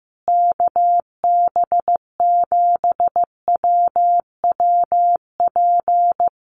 Morse code, heard as a keyed tone, about 15 wpm, 700 Hz. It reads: KB7WWP
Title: Common QSO Phrases Encore 15wpm